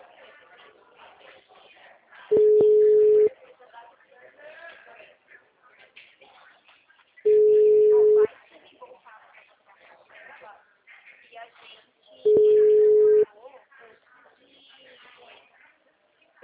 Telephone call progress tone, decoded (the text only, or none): ringback tone